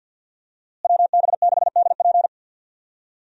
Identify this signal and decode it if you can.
Morse code, heard as a keyed tone, about 52 wpm, 700 Hz. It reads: WB6DP
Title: USA Call Signs - 5 Characters - Repeated 2x - 50wpm